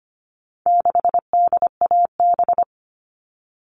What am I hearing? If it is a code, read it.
Morse code, heard as a keyed tone, about 25 wpm, 700 Hz. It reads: 6DAB
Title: All Cut Numbers - 4-Digit 25wpm